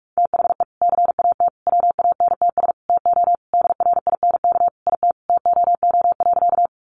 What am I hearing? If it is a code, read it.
Morse code, heard as a keyed tone, about 44 wpm, 700 Hz. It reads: THE CAT WANTS TO DRINK IT TOO.
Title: Additional Easy Phrases 45wpm